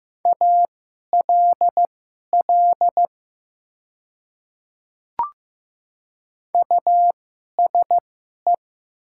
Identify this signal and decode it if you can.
Morse code, heard as a keyed tone, about 15 wpm, spaced as for 11 wpm, 700 Hz. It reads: ALL USE
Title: 2x Character Spacing - Top 100 Words - 15wpm